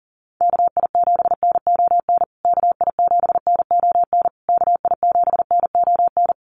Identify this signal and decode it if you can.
Morse code, heard as a keyed tone, about 40 wpm, 700 Hz. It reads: KI7NON KI7NON KI7NON
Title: USA Call Signs Repeated 3x 40wpm